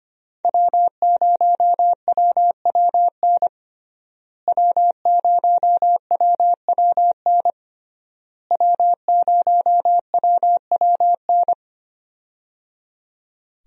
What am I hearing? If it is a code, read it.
Morse code, heard as a keyed tone, about 25 wpm, 700 Hz. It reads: W0WWN W0WWN W0WWN
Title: USA Call Signs - 5 Characters - Repeated 3x - 25wpm